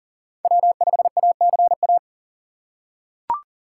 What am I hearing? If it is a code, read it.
Morse code, heard as a keyed tone, about 40 wpm, 700 Hz. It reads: W5ACA
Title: USA Call Signs - 5 Characters - 6x Character Spacing - 40wpm